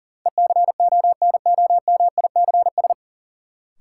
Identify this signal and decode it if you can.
Morse code, heard as a keyed tone, about 40 wpm, 700 Hz. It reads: ECONOMICS
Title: Sentences from Top 1000 Words - Review Part 3 - New Word Drill - 40wpm